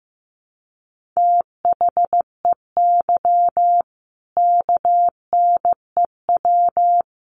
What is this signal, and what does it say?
Morse code, heard as a keyed tone, about 14 wpm, 700 Hz. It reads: THEY KNEW
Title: Sentences from Top 1400 Words 15wpm